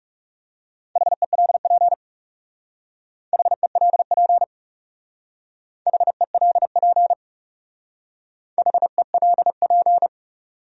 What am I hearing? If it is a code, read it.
Morse code, heard as a keyed tone, about 35 wpm, 700 Hz. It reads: HELP HELP HELP HELP
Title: Speed-Racing Top 200 Words 30wpm